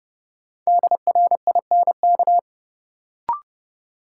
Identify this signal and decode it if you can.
Morse code, heard as a keyed tone, about 30 wpm, 700 Hz. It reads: DRINK
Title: Sentences from Top 1500 Words - New Word Drill - 30wpm